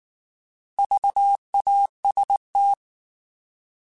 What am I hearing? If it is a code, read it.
Morse code, heard as a keyed tone, about 19 wpm, 800 Hz. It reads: VAST